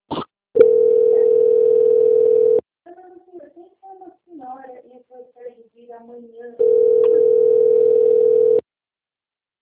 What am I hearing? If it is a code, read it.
Ringback tone (North America), 440 + 480 Hz, with speech.
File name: out- -2028- - - .96